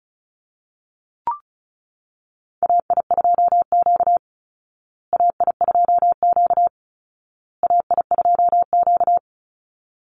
Morse code, heard as a keyed tone, about 34 wpm, 700 Hz. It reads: AI2Q AI2Q AI2Q
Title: Call Signs Repeated 2x - 35wpm